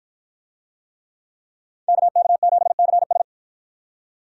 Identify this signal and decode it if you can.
Morse code, heard as a keyed tone, about 53 wpm, 700 Hz. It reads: KK7CS